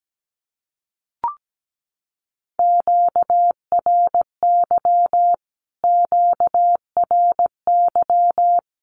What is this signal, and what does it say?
Morse code, heard as a keyed tone, about 17 wpm, 700 Hz. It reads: QRY QRY
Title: Q-Codes Repeated 3x 17wpm